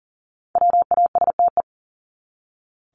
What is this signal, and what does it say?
Morse code, heard as a keyed tone, about 40 wpm, 700 Hz. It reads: WASTE
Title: Top 1751-2000 Words 40wpm